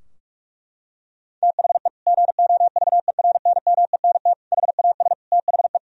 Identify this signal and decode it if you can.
Morse code, heard as a keyed tone, about 46 wpm, 700 Hz. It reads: THE GOVERNMENT HAS THE
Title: Sentences from Top 1000 Words - Review Part 3 - 45wpm